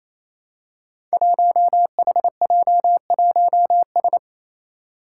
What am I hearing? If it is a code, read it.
Morse code, heard as a keyed tone, about 28 wpm, 700 Hz. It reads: 1HJ1S
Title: Five Challenging Letter-Number Combinations 28wpm